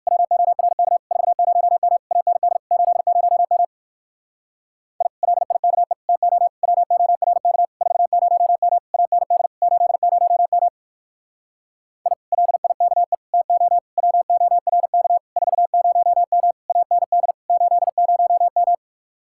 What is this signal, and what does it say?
Morse code, heard as a keyed tone, about 56 wpm, 700 Hz. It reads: WORK 40M AND 80M I LIKE TO WORK 40M AND 80M I LIKE TO WORK 40M AND 80M
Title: Speed-Racing Common QSO Phrases - 45wpm